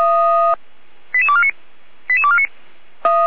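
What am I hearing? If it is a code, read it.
Five-tone selective call (ZVEI), heard as quick runs of five tones, tones of about 70 ms. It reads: 88249 88249
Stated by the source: Received on an SDR; on the server of Feuerwehr Orlamünde.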